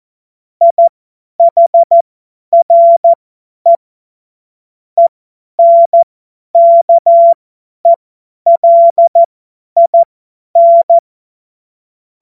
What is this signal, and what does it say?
Morse code, heard as a keyed tone, about 14 wpm, spaced as for 10 wpm, 680 Hz. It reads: IHRE ENKELIN
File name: Morsekurs / Witze / Witz 07. März 2023 50 F70